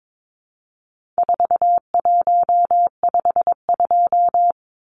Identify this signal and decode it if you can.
Morse code, heard as a keyed tone, about 22 wpm, 700 Hz. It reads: 4152
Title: Numbers - 4-Digit 22wpm